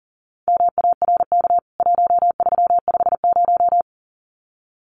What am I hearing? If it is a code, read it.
Morse code, heard as a keyed tone, about 40 wpm, 700 Hz. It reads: MARK 1350
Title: CWT Exchange 40wpm